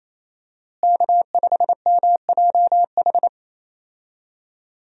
Morse code, heard as a keyed tone, about 28 wpm, 700 Hz. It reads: K5MJH
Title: USA Call Signs - 5 Characters - Repeated 2x - 28wpm